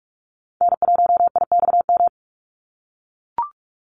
Morse code, heard as a keyed tone, about 44 wpm, 700 Hz. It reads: N1IXM
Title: USA Call Signs 45wpm